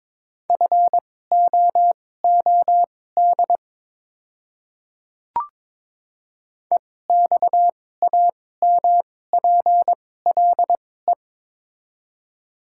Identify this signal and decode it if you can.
Morse code, heard as a keyed tone, about 22 wpm, spaced as for 16 wpm, 700 Hz. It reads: FOOD EXAMPLE
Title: 2x Character Spacing - Top 300 Words - 22wpm